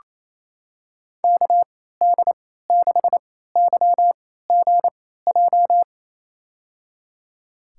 Morse code, heard as a keyed tone, about 28 wpm, spaced as for 16 wpm, 700 Hz. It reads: KD6YGJ
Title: USA Call Signs - 6 Characters - 3x Character Spacing - 28wpm